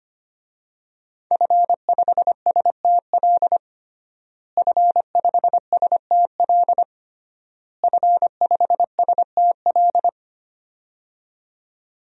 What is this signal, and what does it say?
Morse code, heard as a keyed tone, about 25 wpm, 700 Hz. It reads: F5STL F5STL F5STL
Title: DXCC-100 Call Signs - 5 Characters - Repeated 3x - 25wpm